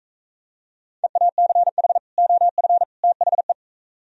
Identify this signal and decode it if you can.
Morse code, heard as a keyed tone, about 41 wpm, 700 Hz. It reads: EACH OF THE